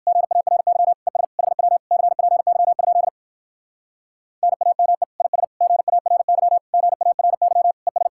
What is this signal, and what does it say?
Morse code, heard as a keyed tone, about 58 wpm, 700 Hz. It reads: GARY ES HW CPY? NAME IS GARY GARY ES